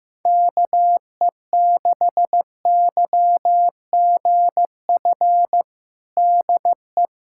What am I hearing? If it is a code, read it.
Morse code, heard as a keyed tone, about 15 wpm, 700 Hz. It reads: KE6YGF DE
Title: USA Callsign DE Callsign - 6 Characters - 15wpm